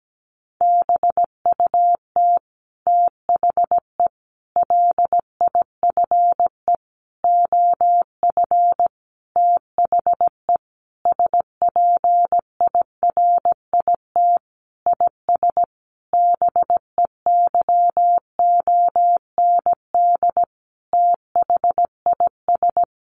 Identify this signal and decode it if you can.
Morse code, heard as a keyed tone, about 17 wpm, 700 Hz. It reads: BUT THE LIFE OF THE SPIRIT IS BEYOND THIS
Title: Sentences from Top 1500 Words 17wpm